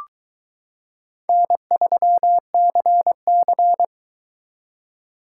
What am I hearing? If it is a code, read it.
Morse code, heard as a keyed tone, about 22 wpm, 700 Hz. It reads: N3CC